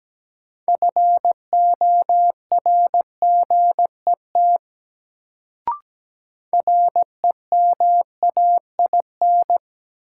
Morse code, heard as a keyed tone, about 17 wpm, 700 Hz. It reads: FORGET REMAIN
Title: Sentences from Top 1200 Words - New Word Drill - 17wpm